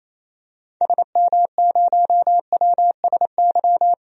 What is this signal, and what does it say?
Morse code, heard as a keyed tone, about 28 wpm, 700 Hz. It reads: SM0WSY
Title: DXCC-100 Call Signs - 6 Characters - 28wpm